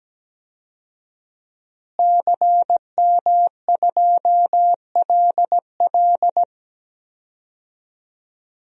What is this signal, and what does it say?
Morse code, heard as a keyed tone, about 17 wpm, 700 Hz. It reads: CM2LL
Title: DXCC-100 Call Signs - 5 Characters - Repeated 2x - 17wpm